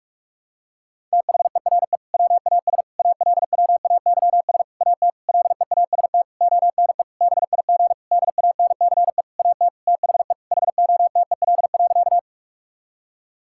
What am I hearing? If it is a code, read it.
Morse code, heard as a keyed tone, about 46 wpm, 700 Hz. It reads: THERE WAS ALWAYS AT LEAST ONE BIG DANCE AT THE HOTEL.
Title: Sentences from Top 1600 Words 45wpm